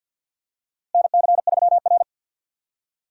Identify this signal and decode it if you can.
Morse code, heard as a keyed tone, about 52 wpm, 700 Hz. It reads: NC3R
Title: USA Call Signs - 4 Characters - Repeated 2x - 50wpm